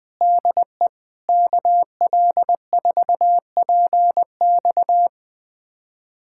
Morse code, heard as a keyed tone, about 20 wpm, 700 Hz. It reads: DE KL4PX
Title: USA Callsign DE Callsign - 5 Characters - 20wpm